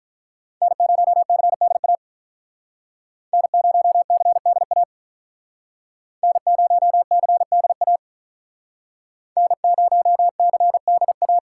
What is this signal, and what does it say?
Morse code, heard as a keyed tone, about 45 wpm, 700 Hz. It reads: N0CDA N0CDA N0CDA N0CDA